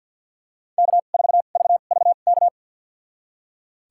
Morse code, heard as a keyed tone, about 46 wpm, spaced as for 34 wpm, 700 Hz. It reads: K4VVK
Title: USA Call Signs - 5 Characters - 2x Character Spacing - 50wpm